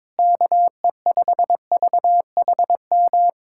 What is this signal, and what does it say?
Morse code, heard as a keyed tone, about 22 wpm, 700 Hz. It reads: KE5VHM
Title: USA Callsign DE Callsign 22wpm